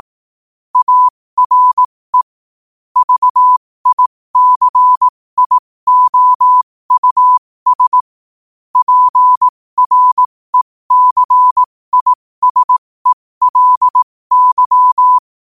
Morse code, heard as a keyed tone, about 18 wpm, spaced as for 15 wpm, 1000 Hz. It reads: ARE VICIOUS PRECISELY